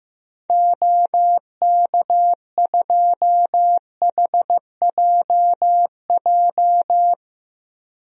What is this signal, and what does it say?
Morse code, heard as a keyed tone, about 16 wpm, 700 Hz. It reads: OK2HJJ
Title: DXCC-100 Call Signs - 6 Characters - Repeated 2x - 15wpm